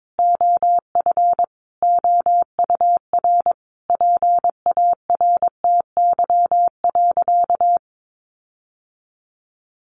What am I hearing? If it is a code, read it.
Morse code, heard as a keyed tone, about 22 wpm, 700 Hz. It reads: OF OUR PARTY.
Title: Sentences from Top 1400 Words 22wpm